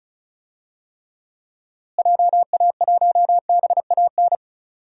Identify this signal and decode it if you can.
Morse code, heard as a keyed tone, about 34 wpm, 700 Hz. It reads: JA1BAN
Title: DXCC-100 Call Signs - 6 Characters - Repeated 2x - 35wpm